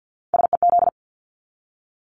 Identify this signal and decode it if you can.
Morse code, heard as a keyed tone, about 50 wpm, 700 Hz. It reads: SEZ
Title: Abbreviations 50wpm